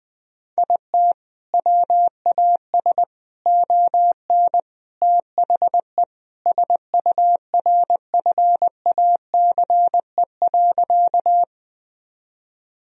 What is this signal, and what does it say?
Morse code, heard as a keyed tone, about 20 wpm, 700 Hz. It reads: IT WAS ON THE SURFACE.
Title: Sentences from Top 1000 Words - Review Part 3 - 20wpm